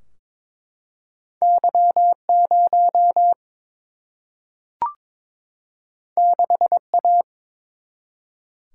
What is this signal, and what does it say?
Morse code, heard as a keyed tone, about 23 wpm, 700 Hz. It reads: Y0 6A
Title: Single-Letter Single-Number 22wpm